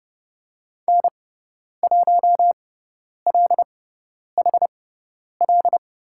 Morse code, heard as a keyed tone, about 30 wpm, spaced as for 10 wpm, 700 Hz. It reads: N1LHL
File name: Lesson 17g - Letter u - Call Signs 30wpm